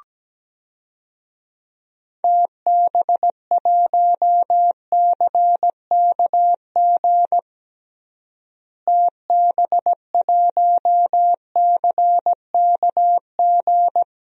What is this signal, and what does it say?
Morse code, heard as a keyed tone, about 17 wpm, 700 Hz. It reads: TB1CKG TB1CKG
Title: DXCC-100 Call Signs - 6 Characters - Repeated 3x - 17wpm